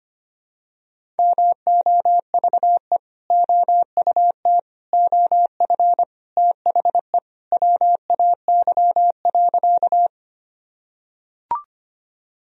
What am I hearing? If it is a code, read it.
Morse code, heard as a keyed tone, about 25 wpm, 700 Hz. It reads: MOVE OUT OF THE WAY.
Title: Sentences from Top 1000 Words - Review Part 1 - 25wpm